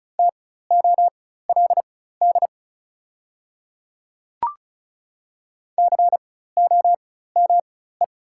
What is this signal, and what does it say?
Morse code, heard as a keyed tone, about 36 wpm, spaced as for 16 wpm, 690 Hz. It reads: TOLD COME
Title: 4x Character Spacing - Top 500 Words - 35wpm